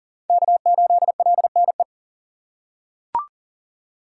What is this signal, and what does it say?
Morse code, heard as a keyed tone, about 40 wpm, 700 Hz. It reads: K8LNE
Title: USA Call Signs - 5 Characters - 2x Character Spacing - 40wpm